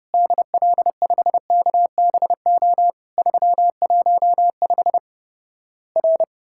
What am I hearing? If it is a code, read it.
Morse code, heard as a keyed tone, about 32 wpm, 700 Hz. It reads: DL5KBO 315 R
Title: Worked All Europe DX Contest - QTC Exchange - 30wpm